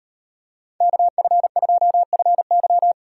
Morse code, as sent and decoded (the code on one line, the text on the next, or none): -.- ..-. ..--- ..-. -.--
KF2FY